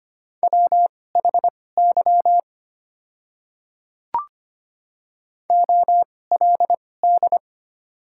Morse code, heard as a keyed tone, about 25 wpm, spaced as for 18 wpm, 700 Hz. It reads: WHY OLD
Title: 2x Character Spacing - Top 200 Words - 25wpm